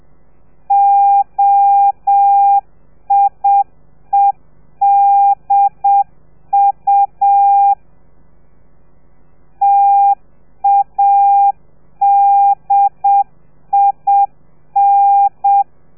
Morse code, heard as a keyed tone, about 7 wpm, 800 Hz. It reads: OIEDU TADIN